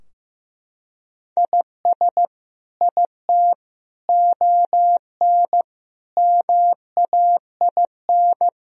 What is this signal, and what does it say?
Morse code, heard as a keyed tone, about 15 wpm, 700 Hz. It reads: IS IT ON MAIN
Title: Sentences from Top 1000 Words - Review Part 2 - 15wpm